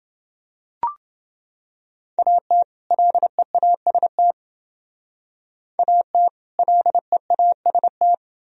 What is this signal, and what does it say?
Morse code, heard as a keyed tone, about 29 wpm, 700 Hz. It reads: AT LEAST AT LEAST